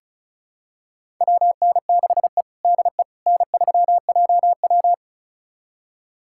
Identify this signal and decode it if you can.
Morse code, heard as a keyed tone, about 36 wpm, 700 Hz. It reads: WN6E DE N3JW
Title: USA Callsign DE Callsign - 4 Characters - 35wpm